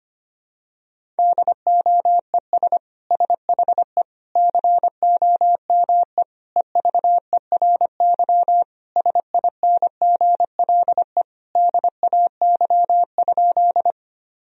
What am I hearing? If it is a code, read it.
Morse code, heard as a keyed tone, about 25 wpm, 700 Hz. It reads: DOES SHE COME EVERY SINGLE DAY?
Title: Sentences from Top 1000 Words - Review Part 4 - 25wpm